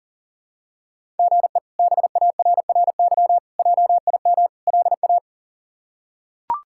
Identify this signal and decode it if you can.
Morse code, heard as a keyed tone, about 40 wpm, 700 Hz. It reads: GE BARRY JIM LA
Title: K1USN SST QSO Practice - 40wpm